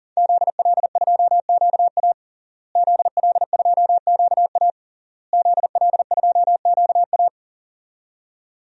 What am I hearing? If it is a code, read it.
Morse code, heard as a keyed tone, about 38 wpm, 700 Hz. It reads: ZL2QA ZL2QA ZL2QA
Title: DXCC-100 Call Signs - 5 Characters - Repeated 3x - 40wpm